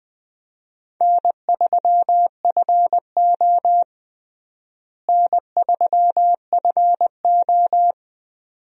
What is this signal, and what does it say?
Morse code, heard as a keyed tone, about 20 wpm, 700 Hz. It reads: N3FO N3FO
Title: USA Call Signs - 4 Characters - Repeated 2x - 20wpm